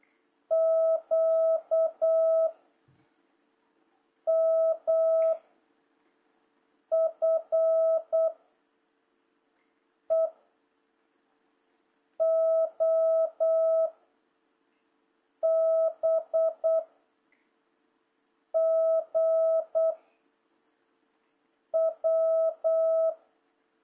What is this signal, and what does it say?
Morse code, heard as a keyed tone, about 8 wpm, 660 Hz. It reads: Q M F E O B G W
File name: jh8cbh / 2026cw